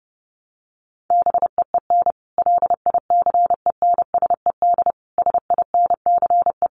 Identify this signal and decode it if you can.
Morse code, heard as a keyed tone, about 30 wpm, 700 Hz. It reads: BEEN LICENSED SINCE
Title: Common QSO Phrases 30wpm